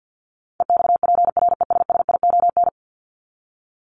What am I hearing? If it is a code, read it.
Morse code, heard as a keyed tone, about 49 wpm, 700 Hz. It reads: EXPRESSION
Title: Top 1001-1250 Words 50wpm